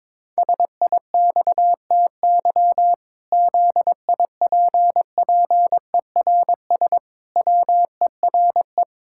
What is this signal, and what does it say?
Morse code, heard as a keyed tone, about 22 wpm, 700 Hz. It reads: SIXTY ZIPPERS WERE
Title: Pangram Challenge v2 - 22wpm